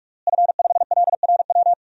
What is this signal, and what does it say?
Morse code, heard as a keyed tone, about 45 wpm, 700 Hz. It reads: F5LRW
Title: DXCC-100 Call Signs - 5 Characters - Repeated 2x - 45wpm